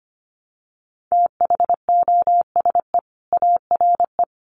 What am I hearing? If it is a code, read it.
Morse code, heard as a keyed tone, about 25 wpm, 700 Hz. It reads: THOSE ARE
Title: Sentences from Top 1100 Words 25wpm